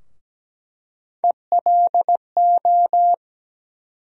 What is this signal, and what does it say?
Morse code, heard as a keyed tone, about 17 wpm, 700 Hz. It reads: ELO
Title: USA Call Sign Postfixes - 3 Letters 17wpm